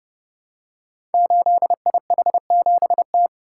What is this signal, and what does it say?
Morse code, heard as a keyed tone, about 30 wpm, 700 Hz. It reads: 8IH7T
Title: Code Groups - 5 Letter:Numbers - 30wpm